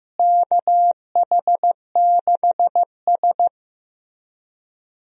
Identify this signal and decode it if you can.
Morse code, heard as a keyed tone, about 15 wpm, 700 Hz. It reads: KH6S
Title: DXCC-100 Call Signs - 4 Characters - Repeated 3x - 15wpm